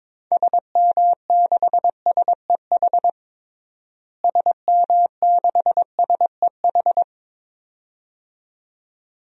Morse code, heard as a keyed tone, about 22 wpm, 700 Hz. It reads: SM6SEH SM6SEH
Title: DXCC-100 Call Signs - 6 Characters - Repeated 2x - 22wpm